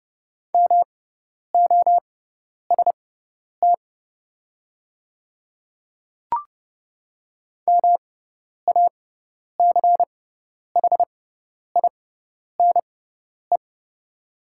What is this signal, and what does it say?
Morse code, heard as a keyed tone, about 30 wpm, spaced as for 10 wpm, 690 Hz. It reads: MOST MACHINE
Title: 6x Character Spacing - Top 500 Words - 30wpm